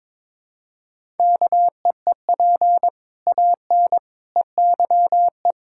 Morse code, heard as a keyed tone, about 22 wpm, 700 Hz. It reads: KEEP AN EYE
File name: practice / 22wpm / Sets of 6 Words - 22wpm